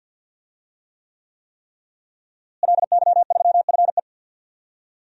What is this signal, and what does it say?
Morse code, heard as a keyed tone, about 53 wpm, 700 Hz. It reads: LY3FE
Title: DXCC-100 Call Signs - 5 Characters - Repeated 3x - 50wpm